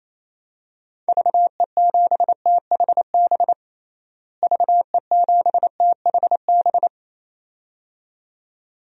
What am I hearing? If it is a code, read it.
Morse code, heard as a keyed tone, about 28 wpm, 700 Hz. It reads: VE7THB VE7THB
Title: DXCC-100 Call Signs - 6 Characters - Repeated 2x - 28wpm